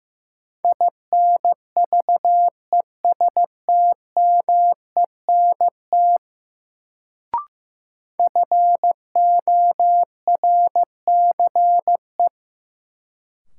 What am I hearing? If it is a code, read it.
Morse code, heard as a keyed tone, about 15 wpm, 700 Hz. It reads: INVESTMENT FORCE
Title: Sentences from Top 700 Words - New Word Drill - 15wpm